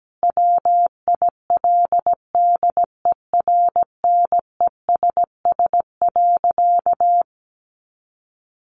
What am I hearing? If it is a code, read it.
Morse code, heard as a keyed tone, about 17 wpm, 700 Hz. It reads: WILDERNESS.
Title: A Sound Of Thunder 17wpm